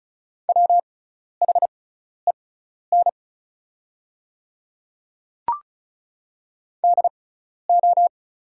Morse code, heard as a keyed tone, about 36 wpm, spaced as for 12 wpm, 690 Hz. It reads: WHEN DO
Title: 6x Character Spacing - Top 500 Words - 35wpm